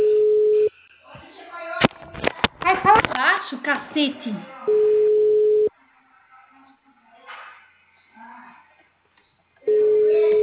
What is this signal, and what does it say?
Ringback tone (Europe, 425 Hz), with speech.
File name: out- -2047- - - .989